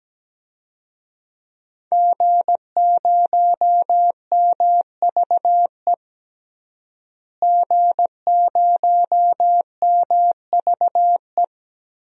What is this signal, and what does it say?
Morse code, heard as a keyed tone, about 17 wpm, 700 Hz. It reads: G0MVE G0MVE
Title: DXCC-100 Call Signs - 5 Characters - Repeated 3x - 17wpm